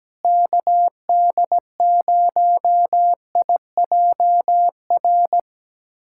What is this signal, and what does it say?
Morse code, heard as a keyed tone, about 17 wpm, 700 Hz. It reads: KD0IJR
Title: USA Callsign DE Callsign - 6 Characters - 17wpm